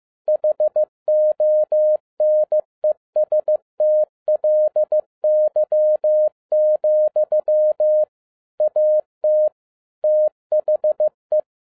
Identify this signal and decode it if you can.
Morse code, heard as a keyed tone, about 15 wpm, 610 Hz. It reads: HONESTLY, AT THE
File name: 2011-07-mp3 / n42 EXCITING QRP CW OPERATION 15wpm